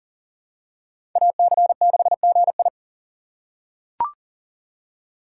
Morse code, heard as a keyed tone, about 40 wpm, 700 Hz. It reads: AC6GI
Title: USA Call Signs - 5 Characters - 3x Character Spacing - 40wpm